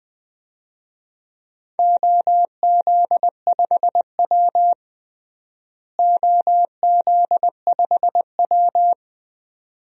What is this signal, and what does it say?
Morse code, heard as a keyed tone, about 21 wpm, 700 Hz. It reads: OZ5W OZ5W
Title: DXCC-100 Call Signs - 4 Characters - Repeated 2x - 20wpm